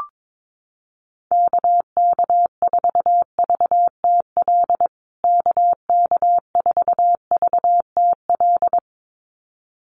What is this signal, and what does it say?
Morse code, heard as a keyed tone, about 22 wpm, 700 Hz. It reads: KK4VTL KK4VTL
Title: USA Call Signs Repeated 2x 22wpm